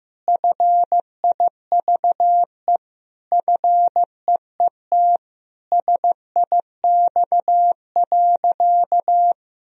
Morse code, heard as a keyed tone, about 15 wpm, 700 Hz. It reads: FIVE FEET SIX.
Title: Sentences from Top 1000 Words - Review Part 2 - 15wpm